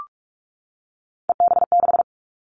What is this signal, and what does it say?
Morse code, heard as a keyed tone, about 44 wpm, 700 Hz. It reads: EB6
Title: Callsign Prefixes for Top 100 DXCC - 45wpm